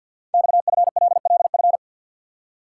Morse code, heard as a keyed tone, about 46 wpm, 700 Hz. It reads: XFLLF
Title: Code Groups - 5 Letters - 50wpm